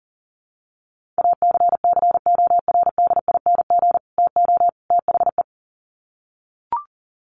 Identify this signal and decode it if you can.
Morse code, heard as a keyed tone, about 40 wpm, 700 Hz. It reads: ACCORDING TO THE
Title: Sets of 3 Words 40wpm